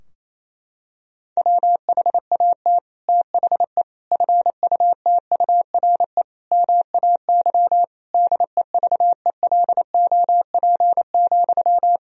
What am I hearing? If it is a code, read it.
Morse code, heard as a keyed tone, about 28 wpm, 700 Hz. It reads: WHAT THE FUTURE MAY DEVELOP,
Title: Sentences from Top 1000 Words - Review Part 2 - 28wpm